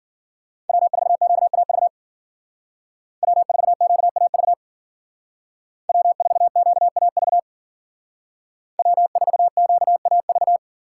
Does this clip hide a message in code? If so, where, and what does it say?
Yes: Morse code from 0.7 to 10.6 s, heard as a keyed tone, about 54 wpm, 700 Hz. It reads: W4QAV W4QAV W4QAV W4QAV